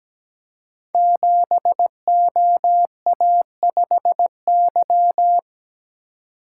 Morse code, heard as a keyed tone, about 17 wpm, 700 Hz. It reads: 7OA5Y
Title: Code Groups - 5 Letter:Numbers - 17wpm